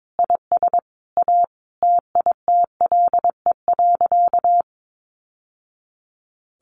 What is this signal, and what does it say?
Morse code, heard as a keyed tone, about 22 wpm, 700 Hz. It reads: IS A TITLE.
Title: Sentences from Top 1200 Words 22wpm